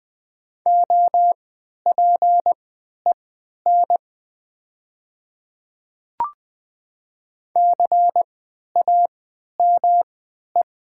Morse code, heard as a keyed tone, about 20 wpm, spaced as for 11 wpm, 700 Hz. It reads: OPEN CAME
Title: 3x Character Spacing - Top 300 Words - 20wpm